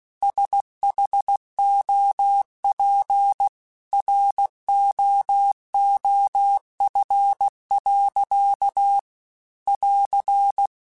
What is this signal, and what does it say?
Morse code, heard as a keyed tone, about 16 wpm, 800 Hz. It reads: SHOP ROOF. <AR>
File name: Weather Sets / Weather Sets-mp3s / Weather 4 24 @15